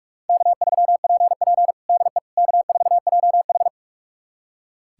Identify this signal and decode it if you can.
Morse code, heard as a keyed tone, about 46 wpm, 700 Hz. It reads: K3PP DE K4JH
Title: USA Callsign DE Callsign - 4 Characters - 45wpm